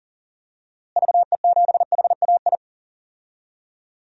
Morse code, heard as a keyed tone, about 40 wpm, 700 Hz. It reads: VE7HAI
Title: DXCC-100 Call Signs - 6 Characters - Repeated 2x - 40wpm